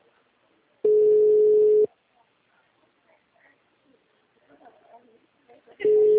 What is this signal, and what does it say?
Ringback tone (Europe, 425 Hz), with speech.